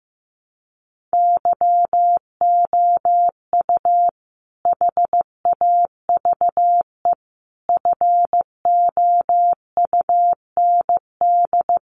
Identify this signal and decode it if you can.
Morse code, heard as a keyed tone, about 15 wpm, 700 Hz. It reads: YOU HAVE FOUND
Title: Sentences from Top 1500 Words 15wpm